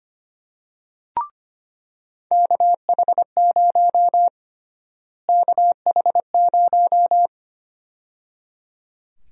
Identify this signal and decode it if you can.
Morse code, heard as a keyed tone, about 25 wpm, 700 Hz. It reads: KH0 KH0
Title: USA Call Sign Prefixes - Repeated 2x - 25wpm